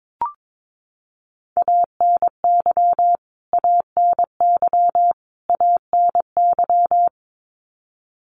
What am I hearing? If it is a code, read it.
Morse code, heard as a keyed tone, about 22 wpm, 700 Hz. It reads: ANY ANY ANY
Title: Top 500 Words Repeated 22wpm